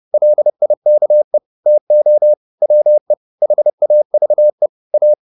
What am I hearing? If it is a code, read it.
Morse code, heard as a keyed tone, about 30 wpm, 600 Hz. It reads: LIKE TO WE HAVE A